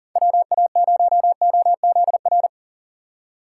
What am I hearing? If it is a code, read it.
Morse code, heard as a keyed tone, about 39 wpm, 700 Hz. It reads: WA0OZR
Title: USA Call Signs - 6 Characters - 3x Character Spacing - 40wpm